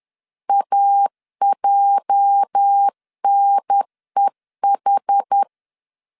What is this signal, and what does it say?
Morse code, heard as a keyed tone, about 10 wpm, 800 Hz. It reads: AJNEH
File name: LEZIONI / 8 YJ45 / 8 YJ 10wpm 2651